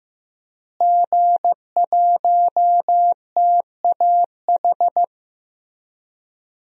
Morse code, heard as a keyed tone, about 15 wpm, 700 Hz. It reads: G1TAH
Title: DXCC-100 Call Signs - 5 Characters - Repeated 3x - 15wpm